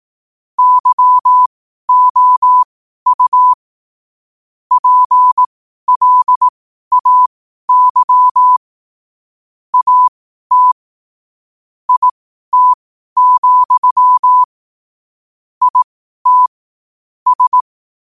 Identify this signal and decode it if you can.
Morse code, heard as a keyed tone, about 18 wpm, spaced as for 12 wpm, 1000 Hz. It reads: YOU PLAY AT IT, IT S
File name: qotdfiles / 12wpm